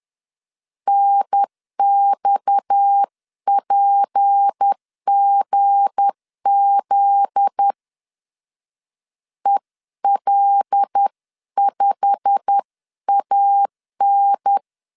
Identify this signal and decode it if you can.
Morse code, heard as a keyed tone, about 10 wpm, 800 Hz. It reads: NXPGZ EL5AN
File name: CW / DET / 1 9 1500C S A 10wpm 4511